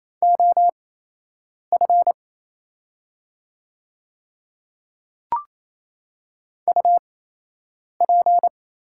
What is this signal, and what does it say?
Morse code, heard as a keyed tone, about 28 wpm, spaced as for 8 wpm, 700 Hz. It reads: OF UP
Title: 8x Character Spacing - Top 100 Words - 28wpm